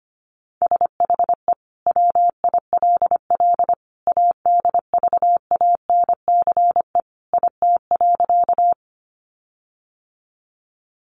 Morse code, heard as a keyed tone, about 25 wpm, 700 Hz. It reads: SHE WILL ADVANCE IT.
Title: Sentences from Top 1400 Words 25wpm